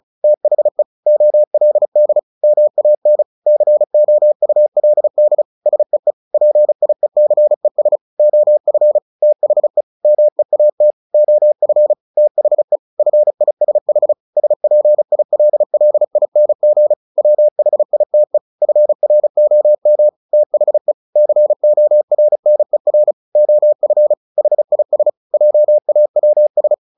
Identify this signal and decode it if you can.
Morse code, heard as a keyed tone, about 34 wpm, 600 Hz. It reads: THE OLD MAN COULD SEE PIECES OF THE MEAT OF THE FISH SPILLING WHITE FROM THE CORNER OF HIS JAWS
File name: the old man and the sea 2 35wpm